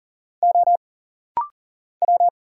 Morse code, heard as a keyed tone, about 40 wpm, 700 Hz. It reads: O W
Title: Single Letter-Number Alt - Rapid-Fire 40wpm